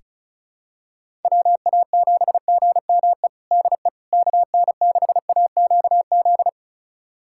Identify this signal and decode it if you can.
Morse code, heard as a keyed tone, about 36 wpm, 700 Hz. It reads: WA7GME DE KN6AQZ
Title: USA Callsign DE Callsign 35wpm